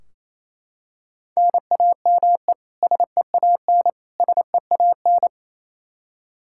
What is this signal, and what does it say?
Morse code, heard as a keyed tone, about 28 wpm, 700 Hz. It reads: NAME SEAN SEAN
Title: Practice QSO - Exchange 1 - Beginning - 28wpm